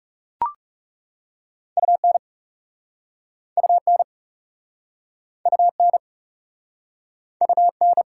Morse code, heard as a keyed tone, about 40 wpm, 690 Hz. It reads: UN UN UN UN